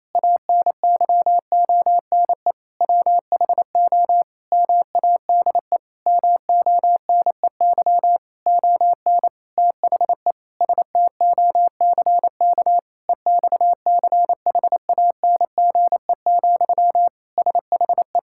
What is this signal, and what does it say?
Morse code, heard as a keyed tone, about 27 wpm, 700 Hz. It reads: ANYONE WHO MADE MONEY ON THE STOCK EXCHANGE, SHE
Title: Sentences from Top 1000 Words - Review Part 6 - 28wpm